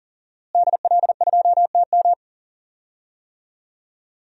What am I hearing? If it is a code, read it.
Morse code, heard as a keyed tone, about 40 wpm, 700 Hz. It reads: DL2TM
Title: DXCC-100 Call Signs - 5 Characters - Repeated 3x - 40wpm